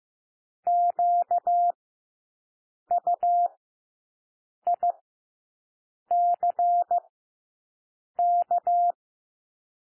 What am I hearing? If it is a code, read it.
Morse code, heard as a keyed tone, about 15 wpm, spaced as for 6 wpm, 700 Hz. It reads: QUICK
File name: cwmp3 / W1 SOUND WORDS 15 wpm